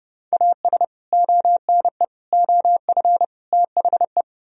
Morse code, heard as a keyed tone, about 29 wpm, 700 Hz. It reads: AS ONE OF THE